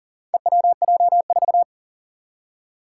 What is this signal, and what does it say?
Morse code, heard as a keyed tone, about 38 wpm, 700 Hz. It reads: EWJ4
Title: Four Challenging Letter-Number Combinations 40wpm